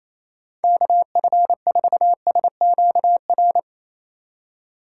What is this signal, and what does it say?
Morse code, heard as a keyed tone, about 35 wpm, 700 Hz. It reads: KF4SQR